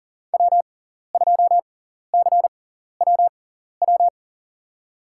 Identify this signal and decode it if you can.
Morse code, heard as a keyed tone, about 40 wpm, spaced as for 14 wpm, 700 Hz. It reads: W2CWW
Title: USA Call Signs - 5 Characters - 6x Character Spacing - 40wpm